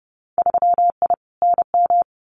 Morse code, heard as a keyed tone, about 30 wpm, 700 Hz. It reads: 3I NM
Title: Winter Field Day Exchanges Encore 30wpm